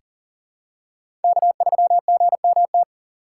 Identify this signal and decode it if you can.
Morse code, heard as a keyed tone, about 40 wpm, 690 Hz. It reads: K3GMT
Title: K1USN SST QSO Practice - 40wpm